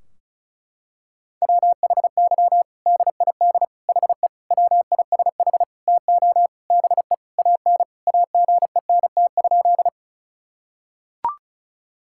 Morse code, heard as a keyed tone, about 34 wpm, 700 Hz. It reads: WHY DID HE WISH TO BE AN AGENT?
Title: Sentences from Top 1600 Words 35wpm